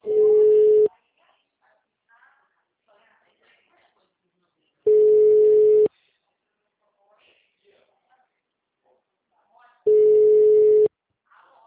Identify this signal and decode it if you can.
Ringback tone (Europe, 425 Hz), with speech.